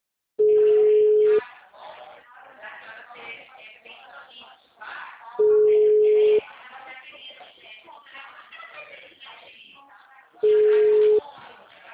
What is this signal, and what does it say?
Ringback tone (Europe, 425 Hz), with speech.